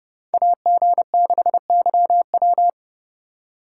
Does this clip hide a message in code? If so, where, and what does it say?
Yes: Morse code from 0.3 to 2.7 s, heard as a keyed tone, about 29 wpm, 700 Hz. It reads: AG6YW